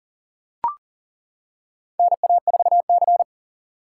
Morse code, heard as a keyed tone, about 40 wpm, 700 Hz. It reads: NA4C
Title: K1USN SST QSO Practice - 40wpm